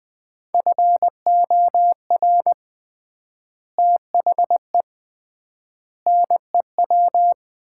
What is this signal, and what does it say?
Morse code, heard as a keyed tone, about 20 wpm, 700 Hz. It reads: FOR THE NEW
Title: 3x Word-Spacing - Sets of 5 Words - 20wpm